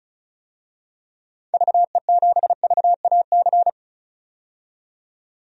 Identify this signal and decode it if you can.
Morse code, heard as a keyed tone, about 36 wpm, 700 Hz. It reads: VE7VAC
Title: DXCC-100 Call Signs - 6 Characters - 35wpm